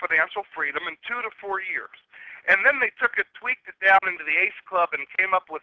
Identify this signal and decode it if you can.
Emergency Alert System (SAME) tones, heard as harsh buzzing bursts, with speech. It is heard from 0.5 to 4.5 s.
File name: calls / july26